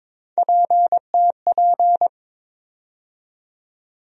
Morse code, heard as a keyed tone, about 22 wpm, 700 Hz. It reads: PTP
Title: USA Call Sign Postfixes - 3 Letters 22wpm